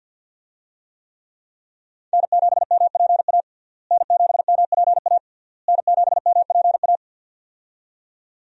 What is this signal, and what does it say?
Morse code, heard as a keyed tone, about 51 wpm, 700 Hz. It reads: N7MPA N7MPA N7MPA
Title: USA Call Signs - 5 Characters - Repeated 3x - 50wpm